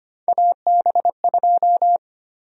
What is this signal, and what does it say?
Morse code, heard as a keyed tone, about 25 wpm, 700 Hz. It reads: AB2
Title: USA Call Sign Prefixes - 25wpm